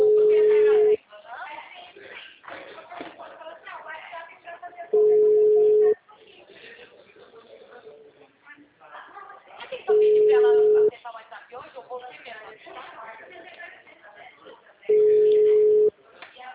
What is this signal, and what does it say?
Ringback tone (Europe, 425 Hz), with speech.
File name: out- -2016- - - .88007